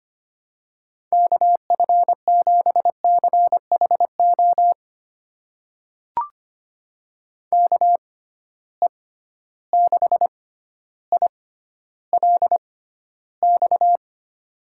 Morse code, heard as a keyed tone, about 25 wpm, spaced as for 13 wpm, 700 Hz. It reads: KF7CHO KE6ILX
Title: USA Call Signs - 6 Characters - 6x Character Spacing - 25wpm